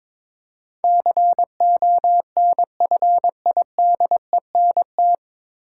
Morse code, heard as a keyed tone, about 21 wpm, 700 Hz. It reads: CONFIDENT
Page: Morse Code Ninja practice 22wpm